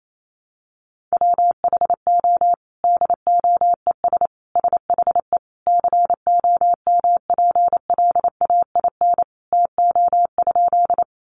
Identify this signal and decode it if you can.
Morse code, heard as a keyed tone, about 28 wpm, 700 Hz. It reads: WHO DOES SHE COMPLAIN TO?
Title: Sentences from Top 1300 Words 28wpm